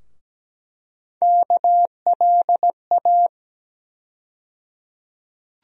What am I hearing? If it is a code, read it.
Morse code, heard as a keyed tone, about 17 wpm, 700 Hz. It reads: KLA
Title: USA Call Sign Postfixes - 3 Letters 17wpm